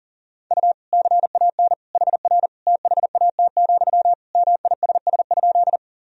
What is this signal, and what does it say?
Morse code, heard as a keyed tone, about 37 wpm, 700 Hz. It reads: U CAN HR THAT, MISS?
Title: Sentences from Top 500 Words and Abbreviations 40wpm